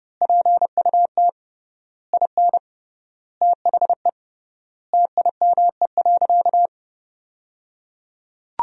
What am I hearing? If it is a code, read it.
Morse code, heard as a keyed tone, about 30 wpm, 700 Hz. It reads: PUT IN THE TIME.
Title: 3x Word-Spacing - Sentences from Top 200 Words - 30wpm